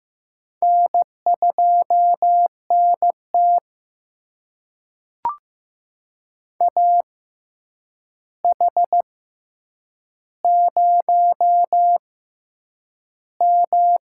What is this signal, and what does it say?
Morse code, heard as a keyed tone, about 15 wpm, spaced as for 8 wpm, 700 Hz. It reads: N2NT AH0M
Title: USA Call Signs - 4 Characters - 6x Character Spacing - 15wpm